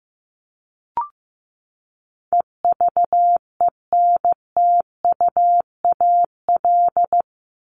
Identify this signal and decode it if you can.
Morse code, heard as a keyed tone, about 15 wpm, 700 Hz. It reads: EVENTUAL
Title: LY Suffix - Top 100 Words with Base Words 15wpm